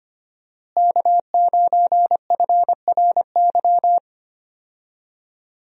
Morse code, heard as a keyed tone, about 25 wpm, 700 Hz. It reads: K9FRY
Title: USA Call Signs - 5 Characters - Repeated 3x - 25wpm